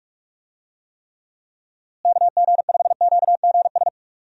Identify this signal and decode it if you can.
Morse code, heard as a keyed tone, about 45 wpm, 700 Hz. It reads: KG5QGS
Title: USA Call Signs - 6 Characters - Repeated 3x - 45wpm